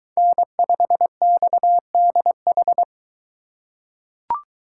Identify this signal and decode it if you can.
Morse code, heard as a keyed tone, about 22 wpm, 700 Hz. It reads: N5XDH